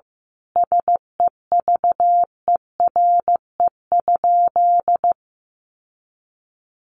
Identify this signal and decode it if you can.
Morse code, heard as a keyed tone, about 15 wpm, 700 Hz. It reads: SEVERE?
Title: Sentences from Top 1400 Words 15wpm